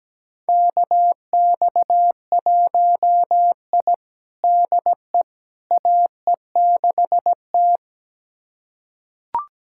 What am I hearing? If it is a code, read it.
Morse code, heard as a keyed tone, about 17 wpm, 700 Hz. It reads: KX1I DE AE6T
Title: USA Callsign DE Callsign - 4 Characters - 17wpm